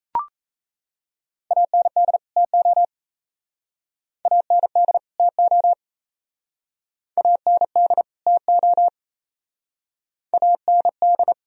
Morse code, heard as a keyed tone, about 37 wpm, 700 Hz. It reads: AND TO AND TO AND TO AND